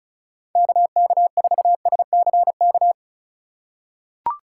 Morse code, heard as a keyed tone, about 36 wpm, 700 Hz. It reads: KK4SCK
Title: USA Call Signs - 6 Characters - 3x Character Spacing - 35wpm